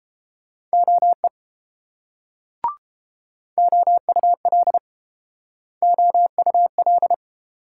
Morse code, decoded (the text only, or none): OE OUL OUL